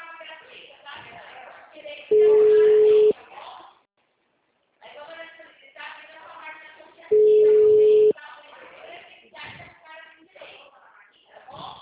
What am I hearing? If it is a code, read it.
Ringback tone (Europe, 425 Hz), with speech.